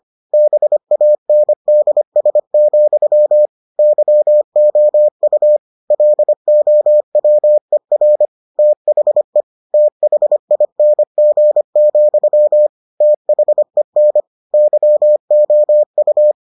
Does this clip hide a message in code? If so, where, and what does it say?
Yes: Morse code from 0.3 to 16.3 s, heard as a keyed tone, about 25 wpm, 600 Hz. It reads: BANDS, YOU LOWER THE THING, THEN YOU